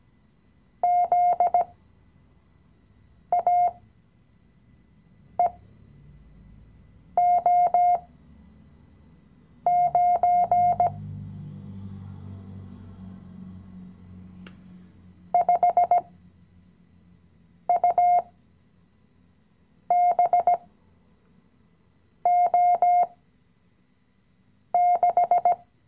Morse code, heard as a keyed tone, about 17 wpm, spaced as for 5 wpm, 710 Hz. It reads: ZAEO9 5UBO6
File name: CW-Online-Kurs / Anfänger / Lektion 018